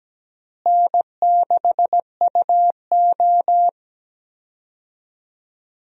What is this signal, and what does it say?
Morse code, heard as a keyed tone, about 17 wpm, 700 Hz. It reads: N6UO
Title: USA Call Signs - 4 Characters - Repeated 3x - 17wpm